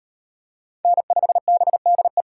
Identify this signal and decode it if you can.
Morse code, heard as a keyed tone, about 37 wpm, 700 Hz. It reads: N5BDE